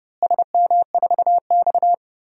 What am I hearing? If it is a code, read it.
Morse code, heard as a keyed tone, about 32 wpm, 700 Hz. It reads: SM4X
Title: DXCC-100 Call Signs - 4 Characters - Repeated 3x - 30wpm